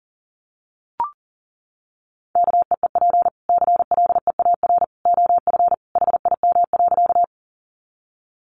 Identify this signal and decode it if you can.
Morse code, heard as a keyed tone, about 40 wpm, 700 Hz. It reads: KEEP CLEAR OF HIM.
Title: Sentences from Top 500 Words 40wpm